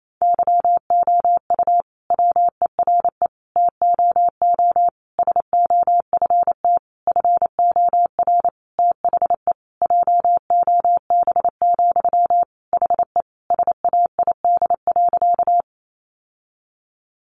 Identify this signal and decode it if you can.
Morse code, heard as a keyed tone, about 28 wpm, 700 Hz. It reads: YOU WERE TOO SOFT FOR THE JOB, HE SAID.
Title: Sentences from Top 1500 Words 28wpm